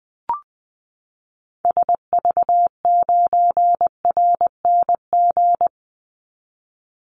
Morse code, heard as a keyed tone, about 20 wpm, 700 Hz. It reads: SV9RNG
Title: Call Signs Repeated 3x - 20wpm